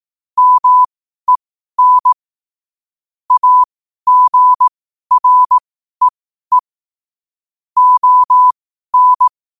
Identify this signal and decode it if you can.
Morse code, heard as a keyed tone, about 18 wpm, spaced as for 12 wpm, 1000 Hz. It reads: MEN AGREE ON
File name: qotdfiles / 12wpm